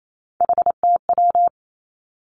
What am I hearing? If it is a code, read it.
Morse code, heard as a keyed tone, about 28 wpm, 700 Hz. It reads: HTW
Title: Three Challenging Letter-Number Combinations 28wpm